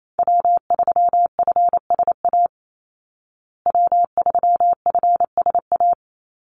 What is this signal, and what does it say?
Morse code, heard as a keyed tone, about 28 wpm, 700 Hz. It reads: W3FSA W3FSA
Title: Call Signs Repeated 2x - 28wpm